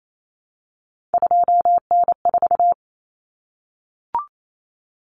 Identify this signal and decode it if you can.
Morse code, heard as a keyed tone, about 28 wpm, 700 Hz. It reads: 2N4
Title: Common Cut Numbers - 3-Digit 28wpm